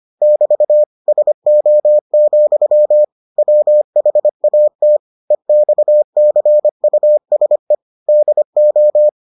Morse code, heard as a keyed tone, about 25 wpm, 600 Hz. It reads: <BT> SO, WHAT EXCUSE DO